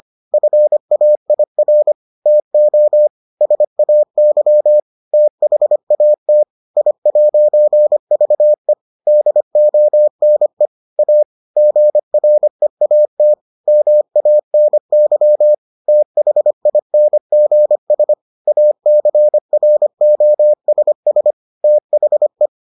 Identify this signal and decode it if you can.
Morse code, heard as a keyed tone, about 25 wpm, 600 Hz. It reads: FAIR TO SAY THAT I'VE DONE A GREAT MANY THINGS ACROSS THE